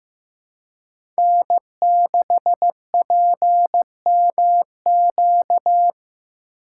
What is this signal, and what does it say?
Morse code, heard as a keyed tone, about 16 wpm, 700 Hz. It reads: N6PMQ
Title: USA Call Signs - 5 Characters - 2x Character Spacing - 15wpm